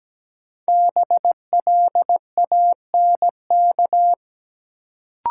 Morse code, heard as a keyed tone, about 17 wpm, 700 Hz. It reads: BLANK
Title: Sentences from Top 1500 Words - New Word Drill - 17wpm